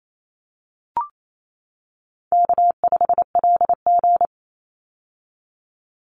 Morse code, heard as a keyed tone, about 28 wpm, 700 Hz. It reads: K5LG
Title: Easy Call Signs - 28wpm